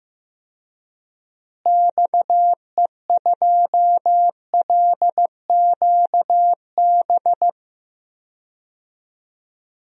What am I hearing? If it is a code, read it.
Morse code, heard as a keyed tone, about 15 wpm, 700 Hz. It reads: XE2LQB
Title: DXCC-100 Call Signs - 6 Characters - Repeated 3x - 15wpm